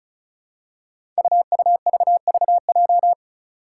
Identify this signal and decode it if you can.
Morse code, heard as a keyed tone, about 36 wpm, 700 Hz. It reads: UUVVJ
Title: Five Challenging Letter-Number Combinations 35wpm